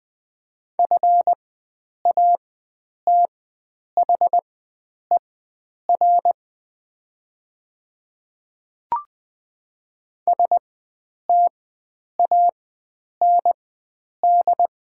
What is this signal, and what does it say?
Morse code, heard as a keyed tone, about 20 wpm, spaced as for 9 wpm, 700 Hz. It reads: FATHER STAND
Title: 4x Character Spacing - Top 300 Words - 20wpm